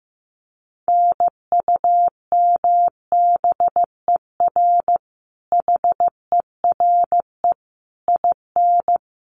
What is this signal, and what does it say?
Morse code, heard as a keyed tone, about 15 wpm, 700 Hz. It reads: NUMBER HERE IN
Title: Sets of 5 Words Encore2 15wpm